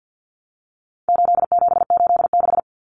Morse code, heard as a keyed tone, about 50 wpm, 700 Hz. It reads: 8786
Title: Numbers - 4-Digit 50wpm